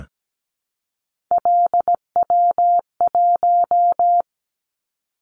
Morse code, heard as a keyed tone, about 17 wpm, 700 Hz. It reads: LW1
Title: Callsign Prefixes for Top 50 DXCC - 17wpm